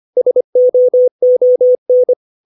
Morse code, heard as a keyed tone, about 25 wpm, 500 Hz. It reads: SOON
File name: 632 words / 632 Words 25WPM